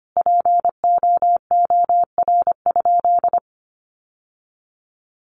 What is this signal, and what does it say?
Morse code, heard as a keyed tone, about 25 wpm, 700 Hz. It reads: POOR?
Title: Sentences from Top 1100 Words 25wpm